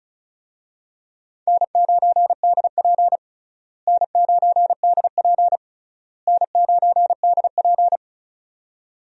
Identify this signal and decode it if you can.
Morse code, heard as a keyed tone, about 36 wpm, 700 Hz. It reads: N9DP N9DP N9DP
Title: USA Call Signs - 4 Characters - Repeated 3x - 35wpm